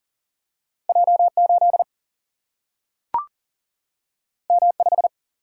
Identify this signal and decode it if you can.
Morse code, heard as a keyed tone, about 40 wpm, 700 Hz. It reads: J8 M5
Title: Single-Letter Single-Number 40wpm